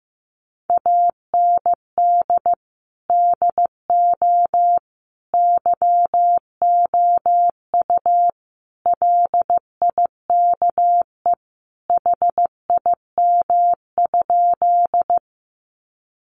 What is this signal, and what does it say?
Morse code, heard as a keyed tone, about 15 wpm, 700 Hz. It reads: AND DO YOU LIKE HIM?
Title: Sentences from Top 100 Words 15wpm